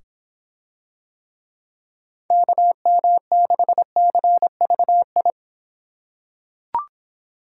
Morse code, heard as a keyed tone, about 25 wpm, 700 Hz. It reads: KM6CVI